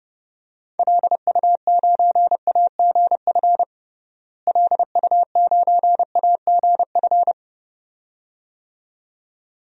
Morse code, heard as a keyed tone, about 30 wpm, 700 Hz. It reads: LU9AGF LU9AGF
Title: DXCC-100 Call Signs - 6 Characters - Repeated 2x - 30wpm